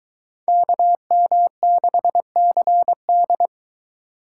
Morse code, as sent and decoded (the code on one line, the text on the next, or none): -.- -- -.... -.-. -..
KM6CD